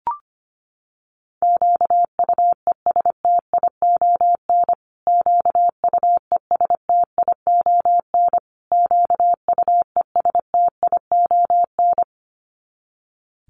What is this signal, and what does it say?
Morse code, heard as a keyed tone, about 25 wpm, 700 Hz. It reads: QUESTION QUESTION QUESTION
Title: Top 500 Words Repeated 25wpm